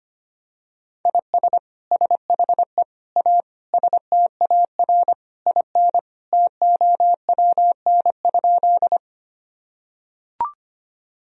Morse code, heard as a keyed tone, about 25 wpm, 700 Hz. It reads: IS SHE A STAR IN TOWN?
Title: Sentences from Top 1000 Words - Review Part 2 - 25wpm